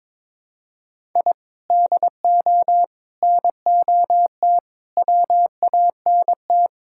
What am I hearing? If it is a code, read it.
Morse code, heard as a keyed tone, about 21 wpm, 700 Hz. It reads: I DO NOT WANT
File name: practice / 22wpm / Build A Phrase - 5-Word Phrases - 22wpm